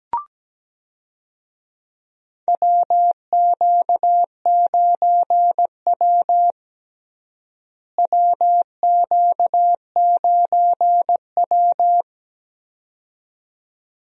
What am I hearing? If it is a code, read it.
Morse code, heard as a keyed tone, about 17 wpm, 700 Hz. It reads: WQ9W WQ9W
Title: USA Call Signs - 4 Characters - Repeated 2x - 17wpm